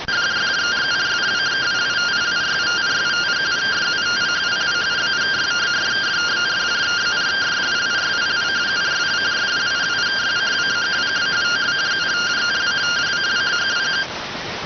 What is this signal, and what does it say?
Two-tone teleprinter signal, about 170 Hz shift.